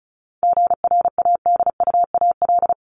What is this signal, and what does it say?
Morse code, heard as a keyed tone, about 34 wpm, 700 Hz. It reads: GRADUAL
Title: LY Suffix - Top 100 Words with Base Words 35wpm